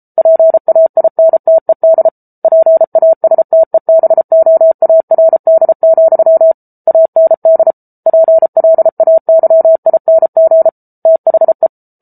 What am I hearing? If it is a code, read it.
Morse code, heard as a keyed tone, about 34 wpm, 650 Hz. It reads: PAINTED PASTEBOARD, AND PLAYING THE